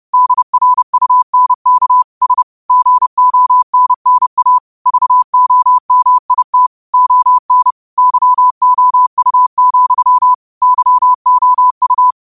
Morse code, heard as a keyed tone, about 30 wpm, 1000 Hz. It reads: DRUNK S GONNA VOMIT ON YOU, YOU
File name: qotdfiles / 30wpm